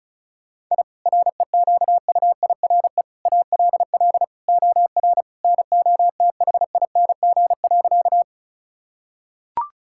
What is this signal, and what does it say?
Morse code, heard as a keyed tone, about 36 wpm, 700 Hz. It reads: I REQUIRE ALL OR NOTHING.
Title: Sentences from Top 1000 Words - Review Part 6 - 35wpm